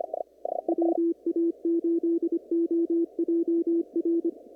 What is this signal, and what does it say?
Morse code, heard as a keyed tone, about 25 wpm, 340 Hz. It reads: WA8OJR
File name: CW / kt5j wpx cw-op-k5pi